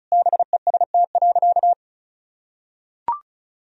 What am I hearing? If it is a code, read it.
Morse code, heard as a keyed tone, about 34 wpm, 700 Hz. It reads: BEST.
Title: Sentences from Top 1000 Words - Review Part 2 - 35wpm